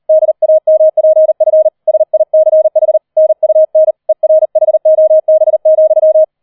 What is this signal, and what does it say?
Morse code, heard as a keyed tone, about 38 wpm, 630 Hz. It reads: DAMPF SICH NUN ERHOB,
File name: 004 Vierter Streich 40wpm